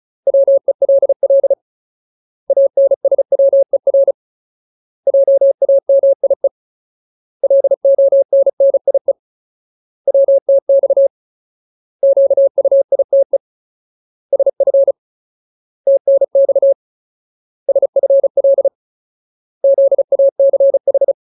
Morse code, heard as a keyed tone, about 34 wpm, 550 Hz. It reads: WELL ANSWER JAMIE LONNIE WTX QUITE SF TNX SFL ZACH
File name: Ultimate Word List / 35WPM Ultimate